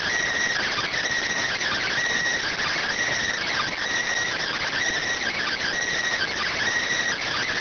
Emergency Alert System (SAME) tones (header bursts), heard as harsh buzzing bursts. It is heard from 0.5 to 6.5 s.